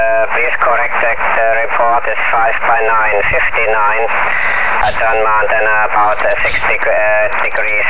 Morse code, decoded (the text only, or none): none